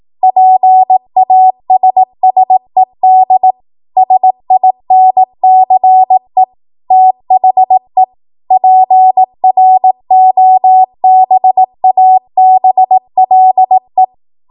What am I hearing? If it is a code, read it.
Morse code, heard as a keyed tone, about 18 wpm, 750 Hz. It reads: PASSED SINCE THE PROBABLE
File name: Morse / 18WPM